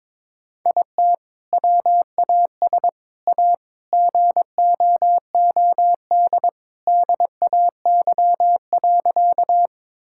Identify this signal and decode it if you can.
Morse code, heard as a keyed tone, about 21 wpm, 700 Hz. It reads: IT WAS A GOOD DAY.
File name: practice / 22wpm / Sentences with 2-Word Phrases - 22wpm